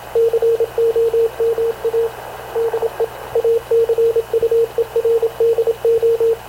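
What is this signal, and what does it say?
Morse code, heard as a keyed tone, about 26 wpm, 460 Hz, with radio noise. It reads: COMA DE ACUERDO
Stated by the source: Havana Radio CLA press (PX) broadcast, recorded by Radio Officer Paul Dumesnil aboard mv CNAV QUEST/CZDO.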